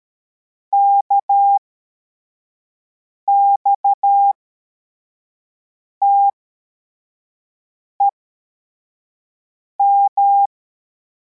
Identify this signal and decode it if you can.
Morse code, heard as a keyed tone, about 13 wpm, spaced as for 4 wpm, 800 Hz. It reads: KXTEM
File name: formation / cw / Text23